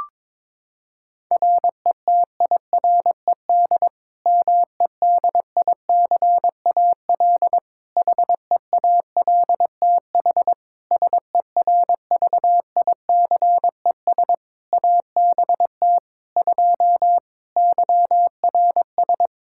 Morse code, heard as a keyed tone, about 22 wpm, 700 Hz. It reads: RETIRED MEDICAL HEALTH SERVICES ABT 2 YRS
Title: Practice QSO - Exchange 3 - More Info - 22wpm